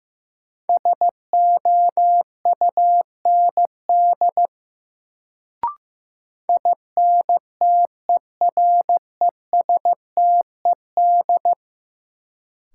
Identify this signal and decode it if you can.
Morse code, heard as a keyed tone, about 15 wpm, 700 Hz. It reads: SOUND INTERESTED
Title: Sentences from Top 900 Words - New Word Drill - 15wpm